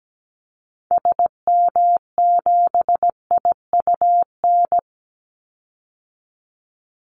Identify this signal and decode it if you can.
Morse code, heard as a keyed tone, about 17 wpm, 700 Hz. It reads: SM7IUN
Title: Call Signs Repeated 3x - 17wpm